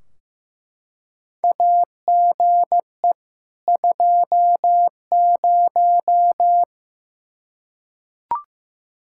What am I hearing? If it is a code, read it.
Morse code, heard as a keyed tone, about 15 wpm, 700 Hz. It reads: AGE 20
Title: Practice QSO - Exchange 3 - More Info - 15wpm